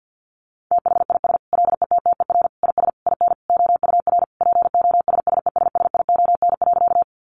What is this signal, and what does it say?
Morse code, heard as a keyed tone, about 50 wpm, 700 Hz. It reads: THIS LETTER IS IN OUR POSSESSION.
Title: Sentences from Top 1400 Words 50wpm